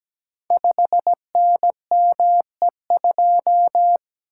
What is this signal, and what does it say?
Morse code, heard as a keyed tone, about 17 wpm, 700 Hz. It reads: 5NME2
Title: Code Groups - 5 Letter:Numbers - 17wpm